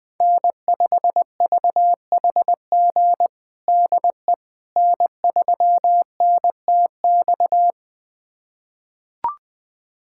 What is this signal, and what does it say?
Morse code, heard as a keyed tone, about 20 wpm, 700 Hz. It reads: N5VHG DE N3NTX
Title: USA Callsign DE Callsign - 5 Characters - 20wpm